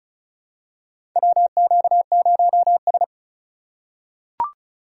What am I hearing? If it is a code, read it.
Morse code, heard as a keyed tone, about 36 wpm, 700 Hz. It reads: WQ0S
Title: USA Call Signs - 4 Characters - 3x Character Spacing - 35wpm